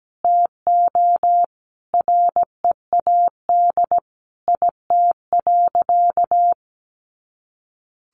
Morse code, heard as a keyed tone, about 17 wpm, 700 Hz. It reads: TO READ IT.
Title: Sentences from Top 500 Words 17wpm